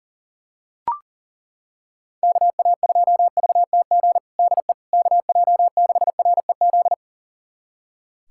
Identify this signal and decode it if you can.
Morse code, heard as a keyed tone, about 40 wpm, 700 Hz. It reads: KA2VTG DE KJ6REZ
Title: USA Callsign DE Callsign - 6 Characters - 40wpm